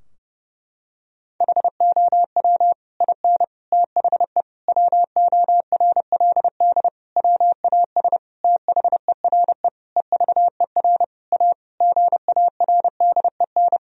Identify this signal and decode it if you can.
Morse code, heard as a keyed tone, about 30 wpm, 700 Hz. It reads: HOW IN THE WORLD WAS THERE EVER A GARDEN
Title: Sentences from Top 1000 Words - Review Part 4 - 30wpm